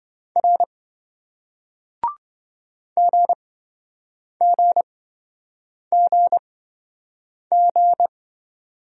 Morse code, heard as a keyed tone, about 28 wpm, spaced as for 7 wpm, 700 Hz. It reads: R GGGG